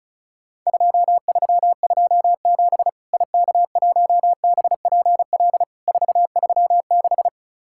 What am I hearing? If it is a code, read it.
Morse code, heard as a keyed tone, about 34 wpm, 700 Hz. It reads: 2327 IK1BPL 436
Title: Worked All Europe DX Contest - QTC Exchange - 35wpm